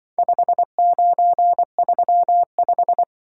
Morse code, heard as a keyed tone, about 25 wpm, 700 Hz. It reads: 5935
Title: Speed-Racing - Four Letter-Number Combinations - 20wpm